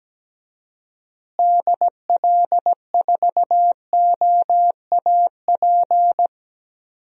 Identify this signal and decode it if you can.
Morse code, heard as a keyed tone, about 17 wpm, 700 Hz. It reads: DL4OAP
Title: USA Call Signs - 6 Characters - Repeated 3x - 17wpm